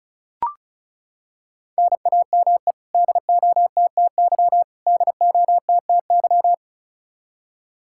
Morse code, heard as a keyed tone, about 36 wpm, 700 Hz. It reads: NAME DOTTY DOTTY
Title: Practice QSO - Exchange 1 - Beginning - 35wpm